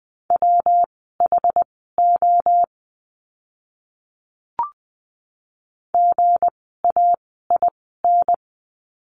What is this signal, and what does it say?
Morse code, heard as a keyed tone, about 20 wpm, spaced as for 15 wpm, 700 Hz. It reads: WHO GAIN
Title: 160 Lesson 26f - Letter g - Words 20wpm